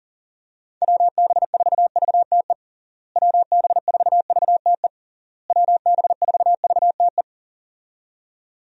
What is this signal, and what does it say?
Morse code, heard as a keyed tone, about 37 wpm, 700 Hz. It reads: WB4VTE WB4VTE WB4VTE
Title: USA Call Signs - 6 Characters - Repeated 3x - 40wpm